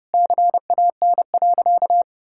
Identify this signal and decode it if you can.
Morse code, heard as a keyed tone, about 32 wpm, 700 Hz. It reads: CAN.
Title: Sentences from Top 1000 Words - Review Part 5 - 30wpm